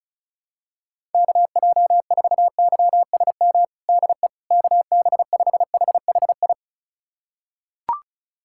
Morse code, heard as a keyed tone, about 36 wpm, 700 Hz. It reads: KJ4YSM DE KB5HHI
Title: USA Callsign DE Callsign - 6 Characters - 35wpm